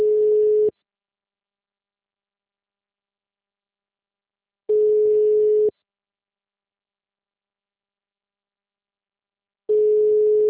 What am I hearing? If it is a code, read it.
Ringback tone (Europe, 425 Hz).